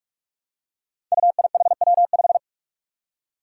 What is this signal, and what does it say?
Morse code, heard as a keyed tone, about 46 wpm, 700 Hz. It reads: UIHW5
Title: Five Challenging Letter-Number Combinations 45wpm